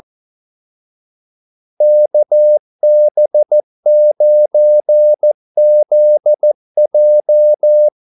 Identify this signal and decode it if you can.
Morse code, heard as a keyed tone, about 14 wpm, 600 Hz. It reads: KB9ZJ
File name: Callsigns-14wpm